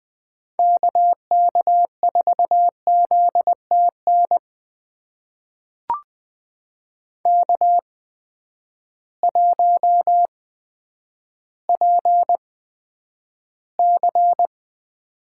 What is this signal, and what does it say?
Morse code, heard as a keyed tone, about 21 wpm, spaced as for 7 wpm, 700 Hz. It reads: KK4ZT NK1PC
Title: USA Call Signs - 8x Character Spacing - 20wpm